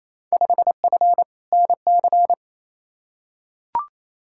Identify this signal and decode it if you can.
Morse code, heard as a keyed tone, about 28 wpm, 700 Hz. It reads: 5F NC
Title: Field Day Exchanges Encore - 28wpm